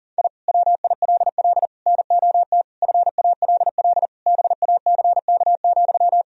Morse code, heard as a keyed tone, about 40 wpm, 700 Hz. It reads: I WILL NOT FALL BACK,
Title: Sentences from Top 1000 Words - Review Part 2 - 40wpm